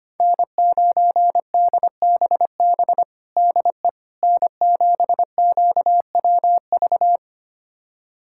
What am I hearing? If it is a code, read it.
Morse code, heard as a keyed tone, about 25 wpm, 700 Hz. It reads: N9DBB DE N7QWV
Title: USA Callsign DE Callsign - 5 Characters - 25wpm